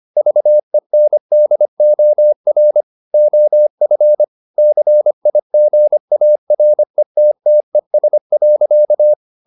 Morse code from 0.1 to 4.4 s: VENDOR OF

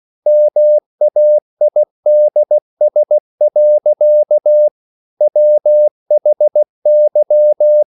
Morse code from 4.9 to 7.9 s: WHY

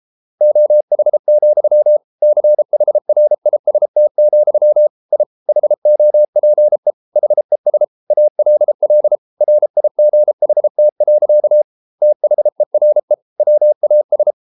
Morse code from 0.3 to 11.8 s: OH, CHRIST, I HOPE HES ALL RIGHT.

